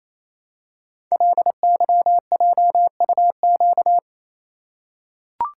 Morse code, heard as a keyed tone, about 28 wpm, 700 Hz. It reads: LYJUQ
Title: Code Groups - 5 Letters - 28wpm